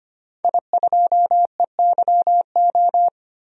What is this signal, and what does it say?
Morse code, heard as a keyed tone, about 25 wpm, 700 Hz. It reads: I2EYO
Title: DXCC-100 Call Signs - 5 Characters - 25wpm